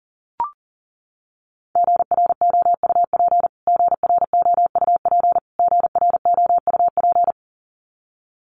Morse code, heard as a keyed tone, about 40 wpm, 700 Hz. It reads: GROUP GROUP GROUP
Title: Top 500 Words Repeated 40wpm